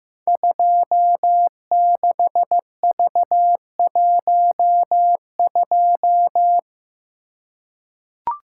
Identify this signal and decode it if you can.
Morse code, heard as a keyed tone, about 15 wpm, 700 Hz. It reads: 26V12
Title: Code Groups - 5 Letter:Numbers - 15wpm